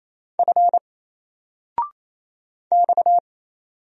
Morse code, heard as a keyed tone, about 28 wpm, 700 Hz. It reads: F X
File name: Single-Letter-Number-28wpm